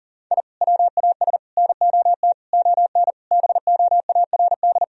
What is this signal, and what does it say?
Morse code, heard as a keyed tone, about 40 wpm, 700 Hz. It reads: I WAS NOT ON BOARD
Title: Sentences from Top 1000 Words - Review Part 4 - 40wpm